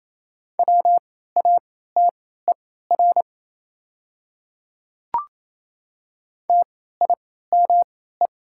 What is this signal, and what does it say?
Morse code, heard as a keyed tone, about 28 wpm, spaced as for 16 wpm, 700 Hz. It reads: WATER TIME
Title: 3x Character Spacing - Top 100 Words - 28wpm